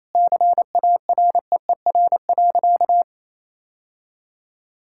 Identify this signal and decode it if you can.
Morse code, heard as a keyed tone, about 28 wpm, 700 Hz. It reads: CAREER.
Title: Sentences from Top 1000 Words - Review Part 6 - 28wpm